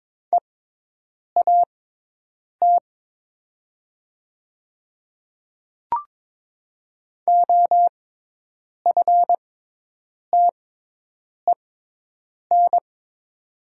Morse code, heard as a keyed tone, about 22 wpm, spaced as for 8 wpm, 700 Hz. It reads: EAT OFTEN
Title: 6x Character Spacing - Top 300 Words - 22wpm